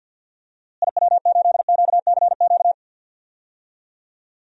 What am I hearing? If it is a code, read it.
Morse code, heard as a keyed tone, about 51 wpm, 700 Hz. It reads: IW8QCQ
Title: DXCC-100 Call Signs - 6 Characters - Repeated 3x - 50wpm